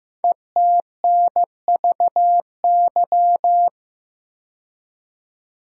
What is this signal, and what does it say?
Morse code, heard as a keyed tone, about 15 wpm, 700 Hz. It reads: ETNVY
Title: Code Groups - 5 Letters - 15wpm